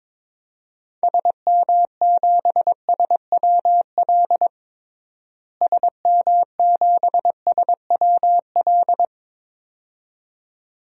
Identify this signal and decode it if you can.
Morse code, heard as a keyed tone, about 22 wpm, 700 Hz. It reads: SM7SWL SM7SWL
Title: DXCC-100 Call Signs - 6 Characters - Repeated 2x - 22wpm